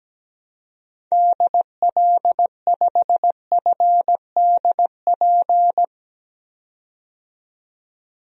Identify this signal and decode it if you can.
Morse code, heard as a keyed tone, about 17 wpm, 700 Hz. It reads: DL5FDP
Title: DXCC-100 Call Signs - 6 Characters - 17wpm